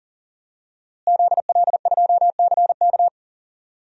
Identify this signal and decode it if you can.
Morse code, heard as a keyed tone, about 40 wpm, 700 Hz. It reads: ZL2CK
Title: DXCC-100 Call Signs - 5 Characters - Repeated 3x - 40wpm